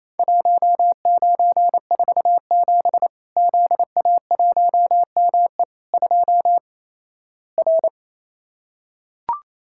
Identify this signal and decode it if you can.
Morse code, heard as a keyed tone, about 28 wpm, 700 Hz. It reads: 1947 ZA1ME 2 R
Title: Worked All Europe DX Contest - QTC Exchange - 28wpm